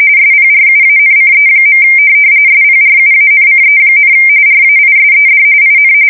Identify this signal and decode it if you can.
RTTY, about 45.45 baud. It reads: SPRUANCE. SPRUANCE, RAISED IN
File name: fsk170 / USS-New-Jersey